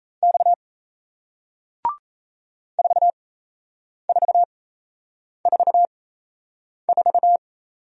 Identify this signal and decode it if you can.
Morse code, heard as a keyed tone, about 38 wpm, spaced as for 8 wpm, 700 Hz. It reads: X 4444